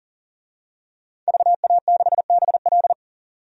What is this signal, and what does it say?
Morse code, heard as a keyed tone, about 40 wpm, 700 Hz. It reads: VA6BL
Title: DXCC-100 Call Signs - 5 Characters - 40wpm